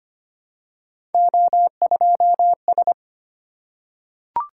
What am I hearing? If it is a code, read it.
Morse code, heard as a keyed tone, about 25 wpm, 700 Hz. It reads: O2S
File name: Three Letter-Number Combinations - 25wpm